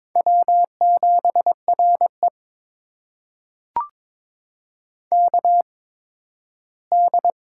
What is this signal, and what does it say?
Morse code, heard as a keyed tone, about 22 wpm, spaced as for 13 wpm, 700 Hz. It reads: W7RE KD
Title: USA Call Signs - 8x Character Spacing - 22wpm